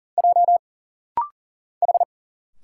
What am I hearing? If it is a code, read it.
Morse code, heard as a keyed tone, about 40 wpm, 700 Hz. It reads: J H
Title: Single Letter-Number Alt - Rapid-Fire 40wpm